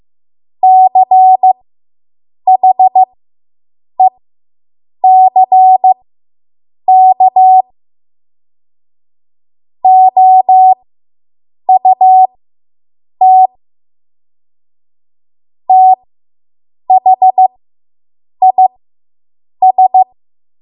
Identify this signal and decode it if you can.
Morse code, heard as a keyed tone, about 15 wpm, spaced as for 7 wpm, 750 Hz. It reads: CHECK OUT THIS